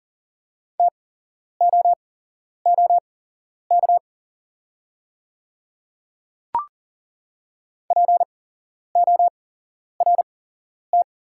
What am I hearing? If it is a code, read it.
Morse code, heard as a keyed tone, about 40 wpm, spaced as for 11 wpm, 700 Hz. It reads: TOOK PORT
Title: 8x Character Spacing - Top 300 Words - 40wpm